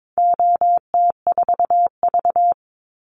Morse code, heard as a keyed tone, about 22 wpm, 700 Hz. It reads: OT4V
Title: Call Signs Repeated 3x - 22wpm